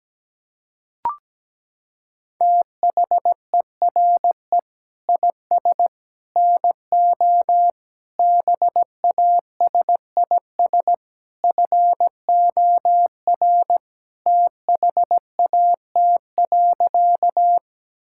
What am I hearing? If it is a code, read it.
Morse code, heard as a keyed tone, about 17 wpm, 700 Hz. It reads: THERE IS NO BASIS FOR THAT.
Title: Sentences from Top 1000 Words - Review Part 5 - 17wpm